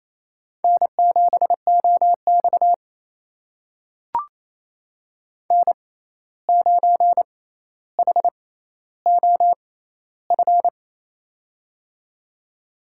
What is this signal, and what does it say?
Morse code, heard as a keyed tone, about 27 wpm, spaced as for 13 wpm, 700 Hz. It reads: N7OX N9HOF
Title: USA Call Signs - 6x Character Spacing - 28wpm